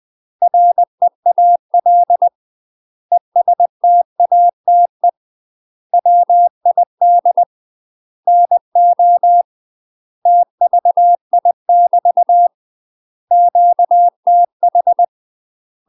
Morse code, heard as a keyed tone, about 20 wpm, 700 Hz. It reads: REAL ESTATE WID NO TVI <BT> QTH
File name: UK MASTER QSOs TRAINING / 20WPM / UK MASTER QSOs 0031